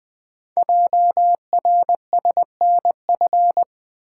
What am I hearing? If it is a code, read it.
Morse code, heard as a keyed tone, about 21 wpm, 700 Hz. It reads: JRSNF
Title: Code Groups - 5 Letters - 20wpm